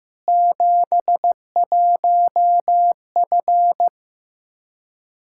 Morse code, heard as a keyed tone, about 16 wpm, 700 Hz. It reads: 71F
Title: K1USN SST QSO Practice 15wpm